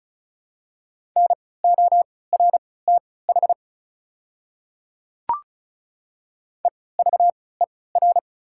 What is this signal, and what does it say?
Morse code, heard as a keyed tone, about 34 wpm, spaced as for 20 wpm, 690 Hz. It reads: NORTH EVER
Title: 3x Character Spacing - Top 500 Words - 35wpm